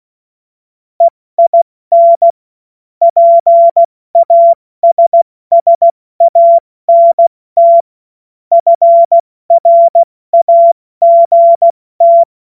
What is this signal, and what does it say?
Morse code, heard as a keyed tone, about 16 wpm, 680 Hz. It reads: EIN PASSANT FRAGT
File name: Morsekurs / Witze / Witz 06. Januar 2026 70 F80